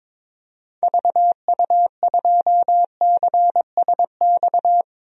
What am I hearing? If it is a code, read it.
Morse code, heard as a keyed tone, about 22 wpm, 700 Hz. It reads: VU2CSX
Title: DXCC-100 Call Signs - 6 Characters - 22wpm